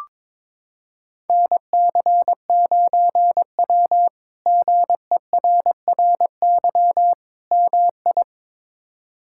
Morse code, heard as a keyed tone, about 21 wpm, 700 Hz. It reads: NC9W GERRY MI
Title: K1USN SST QSO Practice - 22wpm